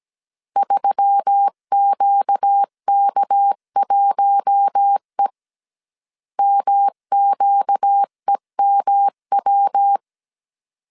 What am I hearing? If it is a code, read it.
Morse code, heard as a keyed tone, about 17 wpm, 800 Hz. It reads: 3QK1E MQEMW
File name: CW / DET / 1 6 1500C S C 16wpm 2733